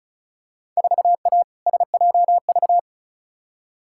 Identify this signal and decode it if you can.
Morse code, heard as a keyed tone, about 36 wpm, 700 Hz. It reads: 4A SJV
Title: Field Day Exchanges Encore - 35wpm